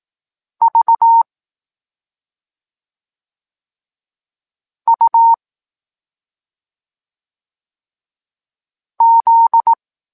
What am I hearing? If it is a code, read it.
Morse code, heard as a keyed tone, about 18 wpm, 920 Hz. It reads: V U Z
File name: Tekst-26 1